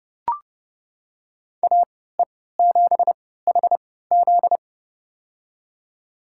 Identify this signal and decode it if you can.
Morse code, heard as a keyed tone, about 30 wpm, spaced as for 17 wpm, 700 Hz. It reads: AE7HZ
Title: USA Call Signs - 5 Characters - 3x Character Spacing - 30wpm